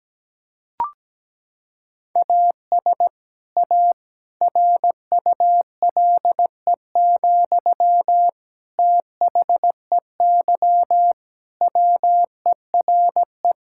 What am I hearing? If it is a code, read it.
Morse code, heard as a keyed tone, about 17 wpm, 700 Hz. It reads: AS A RULE, THEY WERE
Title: Sentences from Top 1000 Words - Review Part 6 - 17wpm